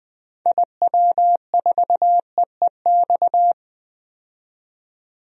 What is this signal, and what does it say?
Morse code, heard as a keyed tone, about 20 wpm, 700 Hz. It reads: IW4EEX
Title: DXCC-100 Call Signs - 6 Characters - 20wpm